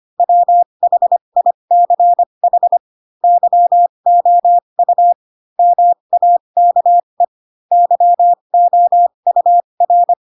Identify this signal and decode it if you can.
Morse code, heard as a keyed tone, about 25 wpm, spaced as for 22 wpm, 700 Hz. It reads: WHICH YOU MAKE YOUR
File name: Pinocchio / Pinocchio ch2 18 25 ik1hge